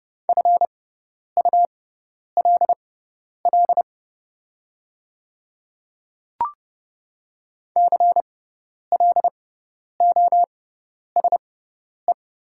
Morse code, heard as a keyed tone, about 30 wpm, spaced as for 10 wpm, 700 Hz. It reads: FULL CLOSE
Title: 6x Character Spacing - Top 500 Words - 30wpm